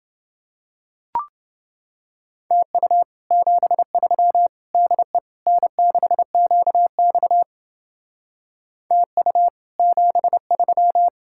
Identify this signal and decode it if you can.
Morse code, heard as a keyed tone, about 28 wpm, 700 Hz. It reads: TU 73 DE N6QX TU 73